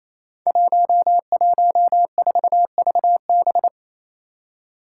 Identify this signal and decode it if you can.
Morse code, heard as a keyed tone, about 28 wpm, 700 Hz. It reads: 114VB
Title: Five Challenging Letter-Number Combinations 28wpm